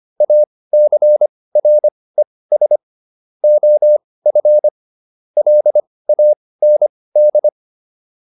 Morse code, heard as a keyed tone, about 25 wpm, spaced as for 18 wpm, 600 Hz. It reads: ACRES OF LAND